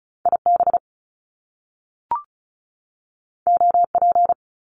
Morse code, heard as a keyed tone, about 36 wpm, 700 Hz. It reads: IB OP
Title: Top Two Letter Combinations in Words 35wpm